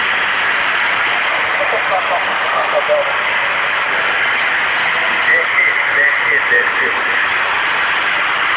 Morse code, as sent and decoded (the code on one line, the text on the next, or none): none